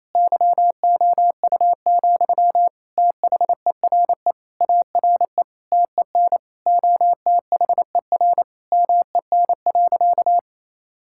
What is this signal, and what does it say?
Morse code, heard as a keyed tone, about 28 wpm, 700 Hz. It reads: YOU, THERE ARE TEN OTHER MEN.
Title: Sentences from Top 1000 Words - Review Part 6 - 28wpm